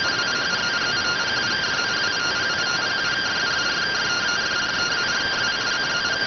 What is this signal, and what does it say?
Two-tone teleprinter signal, about 170 Hz shift.